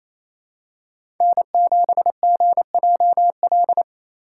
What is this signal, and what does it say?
Morse code, heard as a keyed tone, about 27 wpm, 690 Hz. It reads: N7GJL
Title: USA Call Signs - 3x Character Spacing - 28wpm